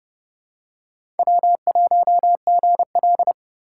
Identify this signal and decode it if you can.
Morse code, heard as a keyed tone, about 29 wpm, 700 Hz. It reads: W1GL
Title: USA Call Signs - 4 Characters - 3x Character Spacing - 30wpm